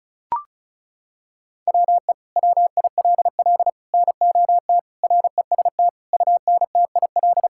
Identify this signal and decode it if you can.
Morse code, heard as a keyed tone, about 34 wpm, 700 Hz. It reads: WE WILL NOT REST UNTIL
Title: Sentences from Top 1600 Words 35wpm